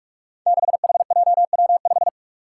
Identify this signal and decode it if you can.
Morse code, heard as a keyed tone, about 44 wpm, 700 Hz. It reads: 6HJW5
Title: Five Challenging Letter-Number Combinations 45wpm